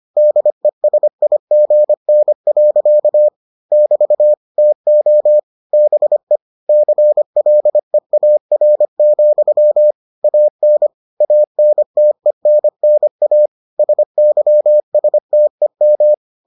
Morse code, heard as a keyed tone, about 25 wpm, 600 Hz. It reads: DESIGN. <BT> TO BE CLEAR, AN ANTENNA SYSTEM